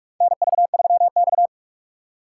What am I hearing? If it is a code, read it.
Morse code, heard as a keyed tone, about 44 wpm, 700 Hz. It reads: NV3X
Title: USA Call Signs - 4 Characters - 6x Character Spacing - 45wpm